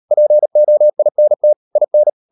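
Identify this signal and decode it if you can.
Morse code, heard as a keyed tone, about 38 wpm, 600 Hz. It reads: POINT IN
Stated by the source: Morse code practice at 38 WPM, sending Ken Follett's novel Never in English, from CW med Gnister.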